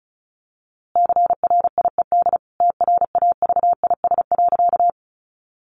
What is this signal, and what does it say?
Morse code, heard as a keyed tone, about 34 wpm, 700 Hz. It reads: CRIED TRAVIS.
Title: A Sound Of Thunder 35wpm